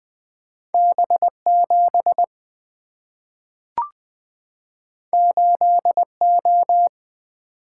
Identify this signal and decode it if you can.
Morse code, heard as a keyed tone, about 20 wpm, 700 Hz. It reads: B7 8O
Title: Two Letter-Number Combinations - 20wpm